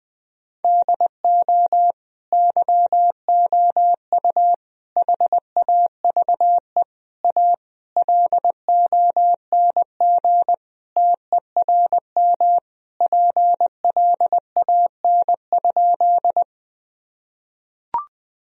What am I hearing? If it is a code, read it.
Morse code, heard as a keyed tone, about 20 wpm, 700 Hz. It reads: DO YOU HAVE A LONG TERM PLAN?
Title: Sentences from Top 1000 Words - Review Part 4 - 20wpm